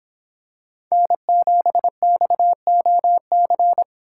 Morse code, heard as a keyed tone, about 26 wpm, 700 Hz. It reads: N7XOC